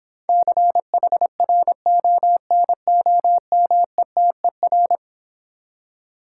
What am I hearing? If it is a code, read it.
Morse code, heard as a keyed tone, about 26 wpm, 700 Hz. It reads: CHRONOMETER